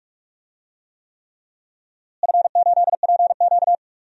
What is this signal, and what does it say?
Morse code, heard as a keyed tone, about 43 wpm, 700 Hz. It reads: F8PQ
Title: DXCC-100 Call Signs - 4 Characters - Repeated 3x - 45wpm